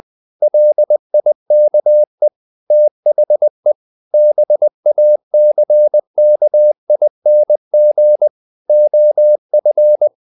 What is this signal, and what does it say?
Morse code, heard as a keyed tone, about 20 wpm, 600 Hz. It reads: LIKE THE BACKING OF
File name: the old man and the sea / the old man and the sea part 1 20wpm